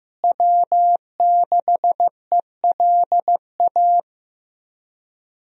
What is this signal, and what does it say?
Morse code, heard as a keyed tone, about 15 wpm, 700 Hz. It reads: W6ELA
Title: USA Call Signs - 5 Characters - Repeated 3x - 15wpm